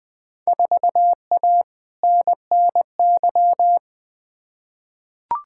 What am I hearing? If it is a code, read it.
Morse code, heard as a keyed tone, about 20 wpm, 700 Hz. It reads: 4A NNY
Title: Field Day Exchanges - 20wpm